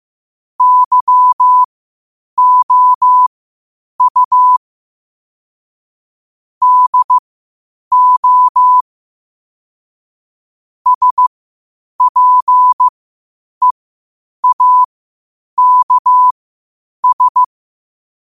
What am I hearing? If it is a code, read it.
Morse code, heard as a keyed tone, about 15 wpm, spaced as for 8 wpm, 1000 Hz. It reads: YOU DO SPEAKS
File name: qotdfiles / 8wpm